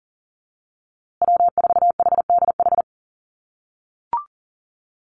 Morse code, heard as a keyed tone, about 40 wpm, 700 Hz. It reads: W4HDH
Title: USA Call Signs Repeated 2x 40wpm